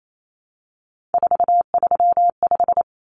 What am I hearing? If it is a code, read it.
Morse code, heard as a keyed tone, about 28 wpm, 700 Hz. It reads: 435
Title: Common Cut Numbers - 3-Digit 28wpm